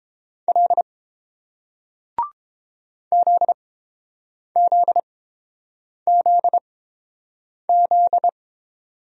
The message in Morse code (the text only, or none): L ZZZZ